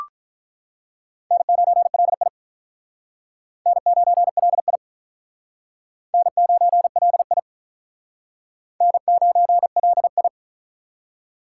Morse code, heard as a keyed tone, about 48 wpm, 700 Hz. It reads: N9LI N9LI N9LI N9LI